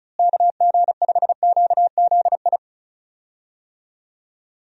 Morse code, heard as a keyed tone, about 36 wpm, 700 Hz. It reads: KG5QZI
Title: USA Call Signs - 6 Characters - Repeated 3x - 35wpm